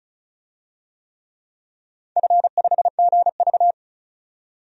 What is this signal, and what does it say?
Morse code, heard as a keyed tone, about 34 wpm, 700 Hz. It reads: F5GV
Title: DXCC-100 Call Signs - 4 Characters - 35wpm